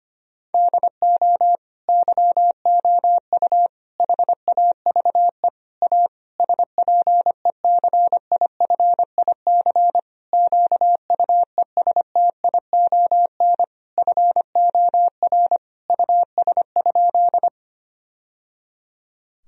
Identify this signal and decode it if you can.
Morse code, heard as a keyed tone, about 25 wpm, 700 Hz. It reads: DO YOU HAVE A SPECIFIC QUESTION FOR US?
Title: Sentences from Top 1000 Words - Review Part 3 - 25wpm